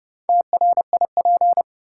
Morse code, heard as a keyed tone, about 32 wpm, 700 Hz. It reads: TRIP
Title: Sentences from Top 1000 Words - Review Part 4 - New Word Drill - 30wpm